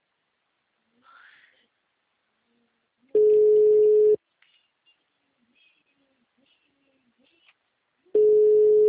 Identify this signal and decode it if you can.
Ringback tone (Europe, 425 Hz).
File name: out- -2056- - - .40936